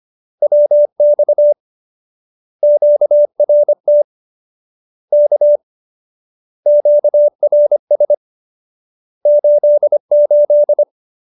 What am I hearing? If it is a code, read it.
Morse code, heard as a keyed tone, about 25 wpm, 590 Hz. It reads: WX QRT K QRS 88